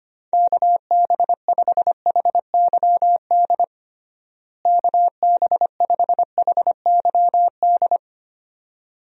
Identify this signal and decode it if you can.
Morse code, heard as a keyed tone, about 25 wpm, 700 Hz. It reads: KB5HYD KB5HYD
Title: USA Call Signs - 6 Characters - Repeated 2x - 25wpm